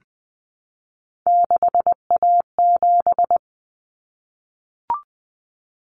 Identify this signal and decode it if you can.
Morse code, heard as a keyed tone, about 20 wpm, 700 Hz. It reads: 6A7
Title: Common Cut Numbers - 3-Digit 20wpm